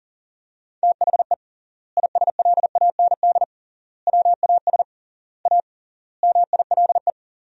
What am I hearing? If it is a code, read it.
Morse code, heard as a keyed tone, about 40 wpm, 700 Hz. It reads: THE ISLAND WAS A MILE
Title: 3x Word-Spacing - Sentences from Top 500 Words - 40wpm